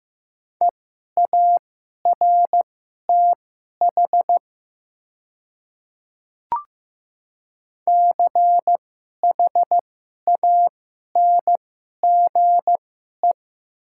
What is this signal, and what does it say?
Morse code, heard as a keyed tone, about 15 wpm, spaced as for 11 wpm, 700 Hz. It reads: EARTH CHANGE
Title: 2x Character Spacing - Top 200 Words - 15wpm